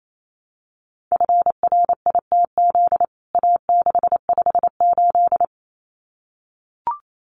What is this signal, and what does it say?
Morse code, heard as a keyed tone, about 28 wpm, 700 Hz. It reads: FRITZ A658
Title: CWT Exchange 28wpm